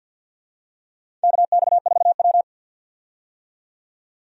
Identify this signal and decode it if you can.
Morse code, heard as a keyed tone, about 53 wpm, 700 Hz. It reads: KX4W
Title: USA Call Signs - 4 Characters - 50wpm